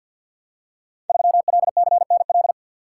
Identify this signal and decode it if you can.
Morse code, heard as a keyed tone, about 50 wpm, 700 Hz. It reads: 3LCNL
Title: Code Groups - 5 Letter:Numbers - 50wpm